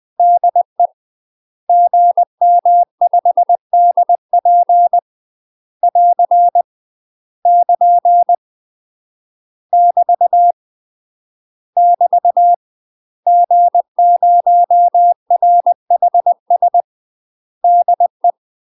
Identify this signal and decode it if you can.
Morse code, heard as a keyed tone, about 20 wpm, 700 Hz. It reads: DE GM5DP <AR> <KN> <BT> <BT> G0RHS DE
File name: UK MASTER QSOs TRAINING / 20WPM / UK MASTER QSOs 0001